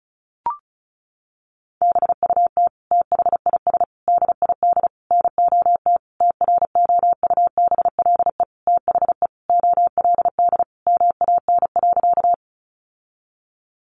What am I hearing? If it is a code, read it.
Morse code, heard as a keyed tone, about 34 wpm, 700 Hz. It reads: BUT THIS DID NOT TROUBLE THE OLD MAN.
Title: Sentences from Top 1200 Words 35wpm